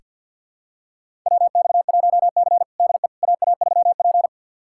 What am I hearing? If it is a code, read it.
Morse code, heard as a keyed tone, about 48 wpm, 700 Hz. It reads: WX1C DE AA3P
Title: USA Callsign DE Callsign - 4 Characters - 50wpm